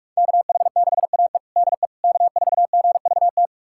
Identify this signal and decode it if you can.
Morse code, heard as a keyed tone, about 46 wpm, 700 Hz. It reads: KH6AE DE K4GVT
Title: USA Callsign DE Callsign - 5 Characters - 45wpm